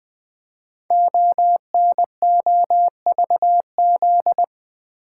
Morse code, heard as a keyed tone, about 20 wpm, 700 Hz. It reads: ONOVZ
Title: Code Groups - 5 Letters - 20wpm